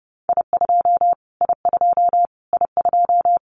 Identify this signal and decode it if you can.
Morse code, heard as a keyed tone, about 30 wpm, 690 Hz. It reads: I2 I2 I2
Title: Callsign Prefixes for Top 25 DXCC - Repeated 3x - 30wpm